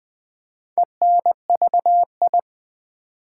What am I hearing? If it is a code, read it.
Morse code, heard as a keyed tone, about 20 wpm, 700 Hz. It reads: ENVI
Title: Build A Word - 4 Letters At A Time - 11-Letter Words - 20wpm